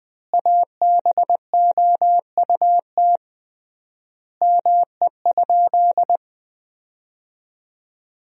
Morse code, heard as a keyed tone, about 21 wpm, 700 Hz. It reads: ABOUT ME?
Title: 3x Word-Spacing - Sentences from Top 200 Words - 20wpm